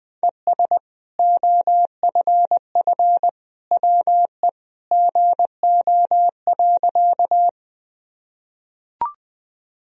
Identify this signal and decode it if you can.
Morse code, heard as a keyed tone, about 20 wpm, 700 Hz. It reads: ES OFF WE GO.
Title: Sentences from Top 200 Words and Abbreviations 20wpm